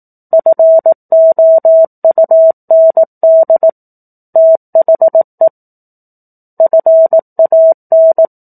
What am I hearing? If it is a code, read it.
Morse code, heard as a keyed tone, about 18 wpm, 650 Hz. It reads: FOUND THE FAN
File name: 04-Chapter-04 18 WPM